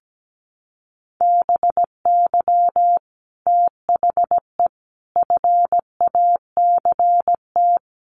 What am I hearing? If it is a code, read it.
Morse code, heard as a keyed tone, about 17 wpm, 700 Hz. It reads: BY THE FACT
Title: Sets of 4 Words 17wpm